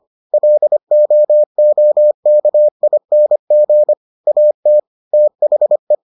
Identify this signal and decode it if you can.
Morse code, heard as a keyed tone, about 25 wpm, 600 Hz. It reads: LOOKING AT THE